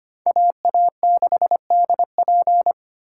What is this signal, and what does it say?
Morse code, heard as a keyed tone, about 34 wpm, 700 Hz. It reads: AA6DP